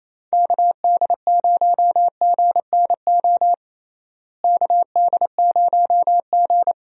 Morse code, heard as a keyed tone, about 28 wpm, 700 Hz. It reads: KD0GNO KD0G
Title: USA Call Signs - 6 Characters - Repeated 2x - 28wpm